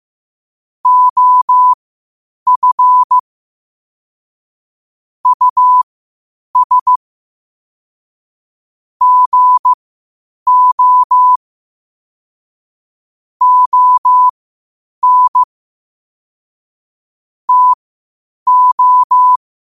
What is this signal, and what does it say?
Morse code, heard as a keyed tone, about 15 wpm, spaced as for 8 wpm, 1000 Hz. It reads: OF US GO ON TO
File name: qotdfiles / 8wpm